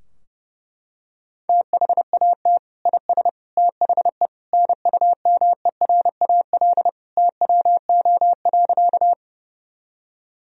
Morse code, heard as a keyed tone, about 30 wpm, 700 Hz. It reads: THAT IS THE NUMERAL TWO.
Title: Sentences from Top 1000 Words - Review Part 2 - 30wpm